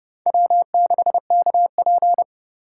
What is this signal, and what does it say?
Morse code, heard as a keyed tone, about 29 wpm, 700 Hz. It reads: W6KP
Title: USA Call Signs - 4 Characters - 3x Character Spacing - 30wpm